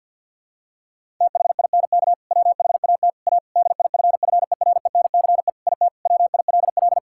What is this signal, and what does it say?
Morse code, heard as a keyed tone, about 50 wpm, 700 Hz. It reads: THINK WHAT A DIFFERENCE IT WILL
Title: Sentences from Top 1000 Words - Review Part 5 - 50wpm